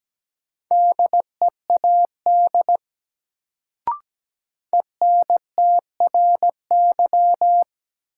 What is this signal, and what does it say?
Morse code, heard as a keyed tone, about 17 wpm, 700 Hz. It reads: DEAD ENTRY
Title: Sentences from Top 1500 Words - New Word Drill Alt - 17wpm